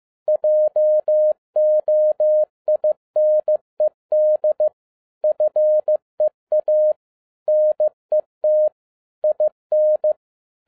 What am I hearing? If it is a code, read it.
Morse code, heard as a keyed tone, about 15 wpm, 610 Hz. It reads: JOINED FEA NET IN
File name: 2012-01-mp3 / n17-SUMMARY OF FEA NET IN 2011-15wpm